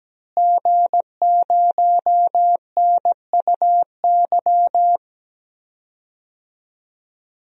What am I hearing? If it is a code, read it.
Morse code, heard as a keyed tone, about 17 wpm, 700 Hz. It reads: G0NUY
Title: DXCC-100 Call Signs - 5 Characters - Repeated 2x - 17wpm